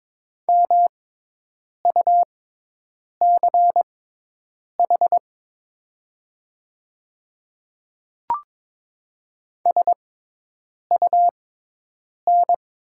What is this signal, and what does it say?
Morse code, heard as a keyed tone, about 22 wpm, spaced as for 8 wpm, 700 Hz. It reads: MUCH SUN
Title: 6x Character Spacing - Top 300 Words - 22wpm